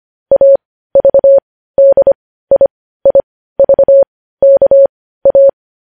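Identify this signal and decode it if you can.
Morse code, heard as a keyed tone, about 25 wpm, spaced as for 15 wpm, 550 Hz. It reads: AVDIIVKA
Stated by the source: Morse code news headlines at 15 WPM.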